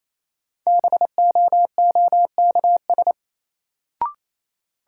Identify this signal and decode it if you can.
Morse code, heard as a keyed tone, about 28 wpm, 700 Hz. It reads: BOOKS
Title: Sentences from Top 1000 Words - Review Part 4 - New Word Drill - 28wpm